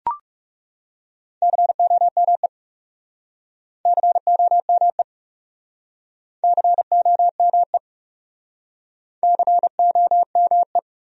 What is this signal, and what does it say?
Morse code, heard as a keyed tone, about 38 wpm, 700 Hz. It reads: COME COME COME COME